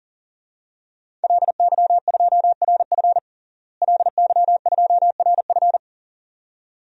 Morse code, heard as a keyed tone, about 40 wpm, 700 Hz. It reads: LY2RF LY2RF
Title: DXCC-100 Call Signs - 5 Characters - Repeated 2x - 40wpm